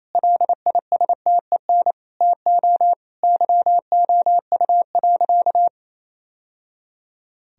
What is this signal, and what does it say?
Morse code, heard as a keyed tone, about 28 wpm, 700 Hz. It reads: LISTEN TO YOU.
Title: Sentences from Top 1000 Words - Review Part 3 - 28wpm